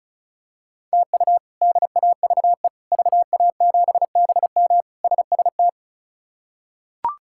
Morse code, heard as a keyed tone, about 36 wpm, 700 Hz. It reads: TU DAVE VA7BM SST
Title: K1USN SST QSO Practice - 35wpm